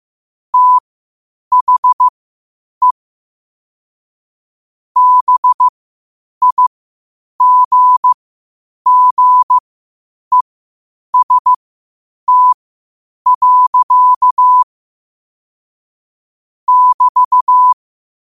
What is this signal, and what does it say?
Morse code, heard as a keyed tone, about 15 wpm, spaced as for 8 wpm, 1000 Hz. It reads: THE BIGGEST. <BT>
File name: qotdfiles / 8wpm